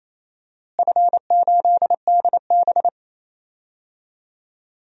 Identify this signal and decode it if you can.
Morse code, heard as a keyed tone, about 28 wpm, 700 Hz. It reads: F8DB
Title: DXCC-100 Call Signs - 4 Characters - Repeated 3x - 28wpm